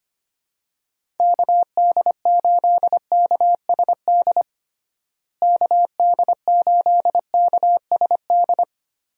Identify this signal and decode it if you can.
Morse code, heard as a keyed tone, about 25 wpm, 700 Hz. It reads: KD8KSD KD8KSD
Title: USA Call Signs - 6 Characters - Repeated 2x - 25wpm